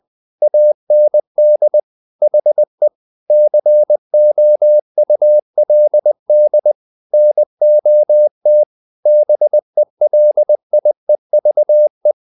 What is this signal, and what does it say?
Morse code, heard as a keyed tone, about 20 wpm, 600 Hz. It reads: AND HE COULD NOT BELIEVE
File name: the old man and the sea part 2 20wpm